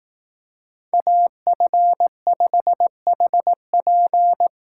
Morse code, heard as a keyed tone, about 18 wpm, 700 Hz. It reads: AF5HP
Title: Speed Racing - USA Call Signs 15wpm